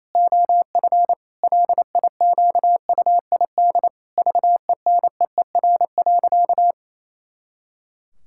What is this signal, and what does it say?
Morse code, heard as a keyed tone, about 28 wpm, 700 Hz. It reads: OF LIQUID VENEER.
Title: Pangram Challenge v2 - 28wpm